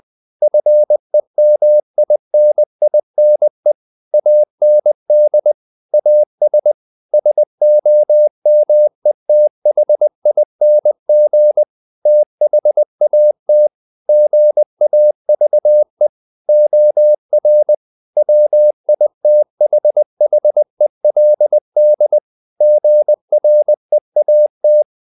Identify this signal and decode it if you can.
Morse code, heard as a keyed tone, about 20 wpm, 600 Hz. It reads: FEMININE AND AS SOMETHING THAT GAVE OR WITHHELD GREAT
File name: the old man and the sea / the old man and the sea part 1 20wpm